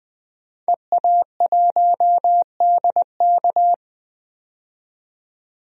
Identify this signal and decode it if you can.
Morse code, heard as a keyed tone, about 20 wpm, 700 Hz. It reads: EA1DK
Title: DXCC-100 Call Signs - 5 Characters - 20wpm